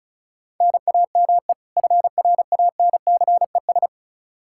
Morse code, heard as a keyed tone, about 36 wpm, 700 Hz. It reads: NAME FRANCES
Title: Practice QSO - Exchange 1 - Beginning - 35wpm